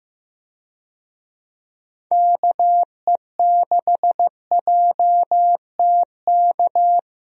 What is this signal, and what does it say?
Morse code, heard as a keyed tone, about 15 wpm, 700 Hz. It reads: KE6JTK
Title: USA Call Signs - 6 Characters - 15wpm